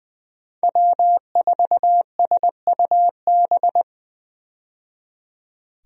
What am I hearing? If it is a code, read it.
Morse code, heard as a keyed tone, about 20 wpm, 700 Hz. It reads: W4SUB
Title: USA Call Signs - 5 Characters - 20wpm